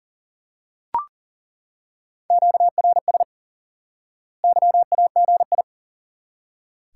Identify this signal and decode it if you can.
Morse code, heard as a keyed tone, about 37 wpm, 700 Hz. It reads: QRS YAGI
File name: IQE Course 3 - Lesson 89 - YAGI - Two Elements - 40wpm